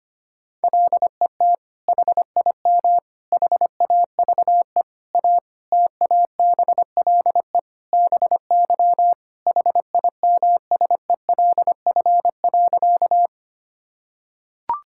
Morse code, heard as a keyed tone, about 25 wpm, 700 Hz. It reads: LET HIM HAVE A TABLE BY HIMSELF.
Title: Sentences from Top 1600 Words 25wpm